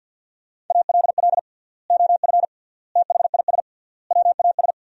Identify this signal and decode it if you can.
Morse code, heard as a keyed tone, about 49 wpm, 700 Hz. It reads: ALL OF THIS WAS
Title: 3x Word-Spacing - Sentences from Top 500 Words - 50wpm